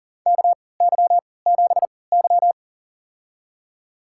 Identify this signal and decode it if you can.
Morse code, heard as a keyed tone, about 37 wpm, spaced as for 22 wpm, 700 Hz. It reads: KY7Y
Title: USA Call Signs - 4 Characters - 3x Character Spacing - 40wpm